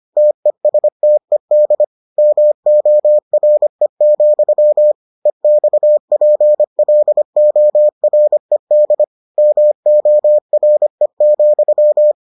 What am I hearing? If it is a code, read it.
Morse code, heard as a keyed tone, about 25 wpm, 600 Hz. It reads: TESTED MORE, EXPLORED MORE,